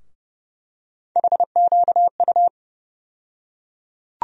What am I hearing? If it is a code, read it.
Morse code, heard as a keyed tone, about 30 wpm, 700 Hz. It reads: HQU
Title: USA Call Sign Postfixes - 3 Letters 30wpm